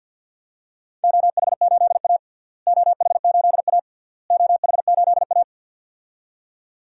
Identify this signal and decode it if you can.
Morse code, heard as a keyed tone, about 49 wpm, 700 Hz. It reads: OH8A OH8A OH8A
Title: DXCC-100 Call Signs - 4 Characters - Repeated 3x - 50wpm